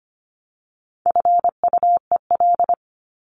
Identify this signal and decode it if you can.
Morse code, heard as a keyed tone, about 25 wpm, 700 Hz. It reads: FUEL
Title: Top 1751-2000 Words 25wpm